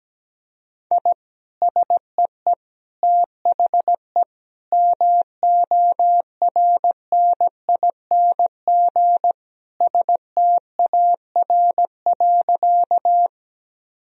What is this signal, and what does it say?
Morse code, heard as a keyed tone, about 17 wpm, 700 Hz. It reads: I SEE THE MORNING STAR.
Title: Sentences from Top 1000 Words - Review Part 2 - 17wpm